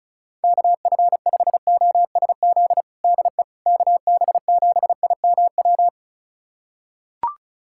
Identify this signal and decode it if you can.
Morse code, heard as a keyed tone, about 36 wpm, 700 Hz. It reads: KF5OSZ DE KB7IMW
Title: USA Callsign DE Callsign 35wpm